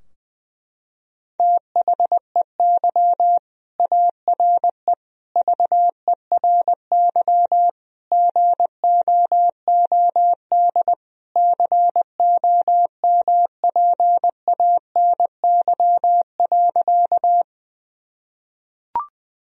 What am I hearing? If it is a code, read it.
Morse code, heard as a keyed tone, about 20 wpm, 700 Hz. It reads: THEY ARE VERY GOOD COMPANY.
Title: Sentences from Top 1000 Words - Review Part 3 - 20wpm